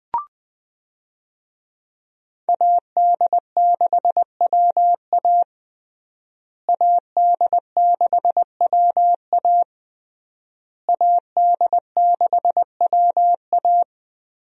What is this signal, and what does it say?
Morse code, heard as a keyed tone, about 21 wpm, 700 Hz. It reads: AD6WA AD6WA AD6WA
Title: USA Call Signs - 5 Characters - Repeated 3x - 20wpm